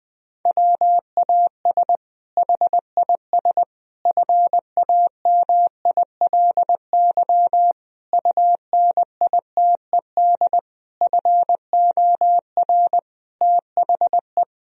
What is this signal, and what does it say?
Morse code, heard as a keyed tone, about 21 wpm, 700 Hz. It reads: WAS HIS FAMILY UNITED FOR THE
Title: Sentences from Top 1000 Words - Review Part 4 - 20wpm